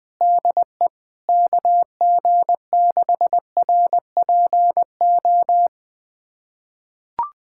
Morse code, heard as a keyed tone, about 20 wpm, 700 Hz. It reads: DE KG6RPO
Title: USA Callsign DE Callsign 20wpm